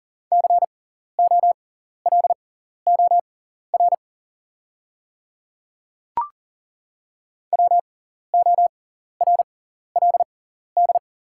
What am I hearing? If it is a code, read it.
Morse code, heard as a keyed tone, about 40 wpm, spaced as for 14 wpm, 700 Hz. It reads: COLOR WORLD
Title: 6x Character Spacing - Top 300 Words - 40wpm